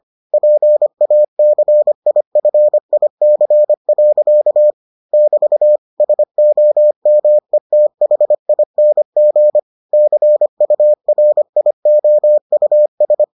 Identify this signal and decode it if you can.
Morse code, heard as a keyed tone, about 25 wpm, 600 Hz. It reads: PACIFIC. <BT> SOMETHING CURIOUS